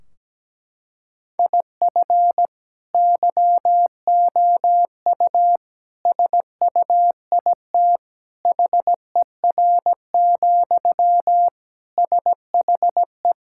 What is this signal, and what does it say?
Morse code, heard as a keyed tone, about 17 wpm, 700 Hz. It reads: IF YOU SUIT HER, SHE
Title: Sentences from Top 1600 Words 17wpm